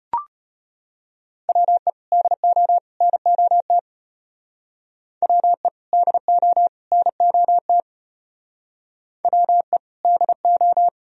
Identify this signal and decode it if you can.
Morse code, heard as a keyed tone, about 33 wpm, 700 Hz. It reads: WE DO NOT WE DO NOT WE DO